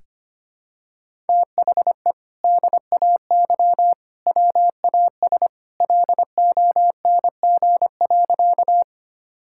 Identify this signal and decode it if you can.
Morse code, heard as a keyed tone, about 25 wpm, 700 Hz. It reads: THE DAY WAS LONG.
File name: practice / 25wpm / Sentences with 2-Word Phrases - 25wpm